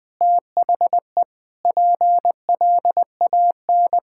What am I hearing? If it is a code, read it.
Morse code, heard as a keyed tone, about 20 wpm, 700 Hz. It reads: THE PLAN
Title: Sentences with 8-Word Phrases - 20wpm